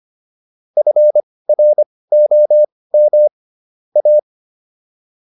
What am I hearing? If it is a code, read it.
Morse code, heard as a keyed tone, about 25 wpm, spaced as for 18 wpm, 600 Hz. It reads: FROM A
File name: WD104 18